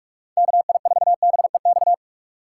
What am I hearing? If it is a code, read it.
Morse code, heard as a keyed tone, about 44 wpm, 700 Hz. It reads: KI4BEX
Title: USA Call Signs - 6x Character Spacing - 45wpm